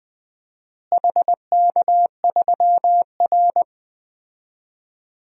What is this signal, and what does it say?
Morse code, heard as a keyed tone, about 20 wpm, 700 Hz. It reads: HK3R
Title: DXCC-100 Call Signs - 4 Characters - Repeated 3x - 20wpm